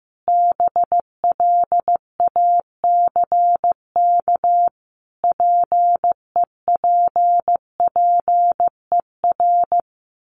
Morse code, heard as a keyed tone, about 15 wpm, 700 Hz. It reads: BLACK PEPPER
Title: Sets of 5 Words Encore 15wpm